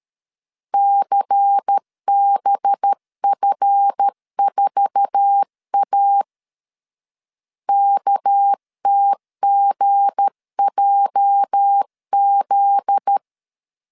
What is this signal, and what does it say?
Morse code, heard as a keyed tone, about 13 wpm, 800 Hz. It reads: CBF4A KTGJZ
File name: LEZIONI / 9 286 12wpm 2305